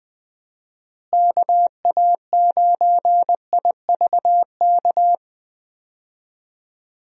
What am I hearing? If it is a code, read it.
Morse code, heard as a keyed tone, about 20 wpm, 700 Hz. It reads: KA9IVK
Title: USA Call Signs - 6 Characters - Repeated 2x - 20wpm